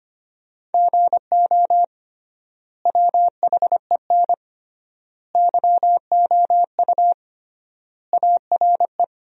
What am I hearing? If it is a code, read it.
Morse code, heard as a keyed tone, about 25 wpm, 700 Hz. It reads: GO WHEN YOU ARE
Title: 3x Word-Spacing - Sentences from Top 500 Words - 25wpm